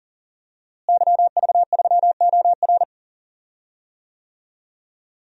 Morse code, heard as a keyed tone, about 40 wpm, 700 Hz. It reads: YV3OR
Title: DXCC-100 Call Signs - 5 Characters - Repeated 3x - 40wpm